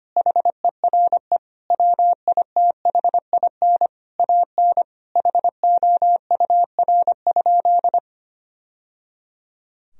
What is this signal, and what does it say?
Morse code, heard as a keyed tone, about 25 wpm, 700 Hz. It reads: HERE WITHIN AN HOUR?
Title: Sentences from Top 1000 Words - Review Part 4 - 25wpm